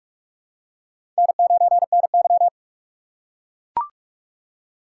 Morse code, heard as a keyed tone, about 44 wpm, 700 Hz. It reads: N9NY
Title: USA Call Signs - 4 Characters - 2x Character Spacing - 45wpm